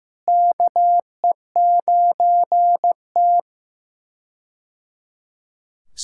Morse code, heard as a keyed tone, about 15 wpm, 700 Hz. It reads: KE9T
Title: K1USN SST QSO Practice - 15wpm